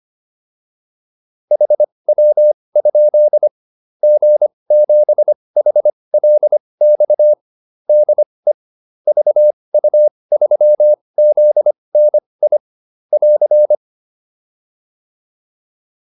Morse code, heard as a keyed tone, about 25 wpm, spaced as for 20 wpm, 600 Hz. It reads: HW? G7HLX DE VU3ZNI <AR>